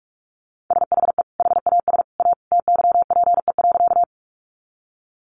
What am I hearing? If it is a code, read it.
Morse code, heard as a keyed tone, about 44 wpm, 700 Hz. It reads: SHE HAS A TYPE.
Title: Sentences from Top 600 Words 45wpm